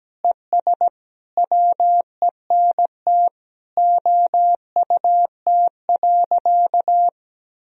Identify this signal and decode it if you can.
Morse code, heard as a keyed tone, about 17 wpm, 700 Hz. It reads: ES WENT OUT.
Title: Sentences from Top 300 Words and Abbreviations 17wpm